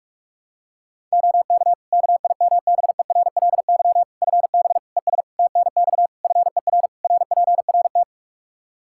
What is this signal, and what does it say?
Morse code, heard as a keyed tone, about 41 wpm, 700 Hz. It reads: OK KIMBERLY FB ES TNX FER RPRT
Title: Practice QSO - Exchange 2 - Info - 45wpm